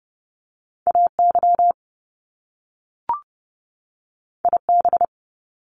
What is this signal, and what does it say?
Morse code, heard as a keyed tone, about 30 wpm, 700 Hz. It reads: AY IB
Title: Top Two Letter Combinations in Words 30wpm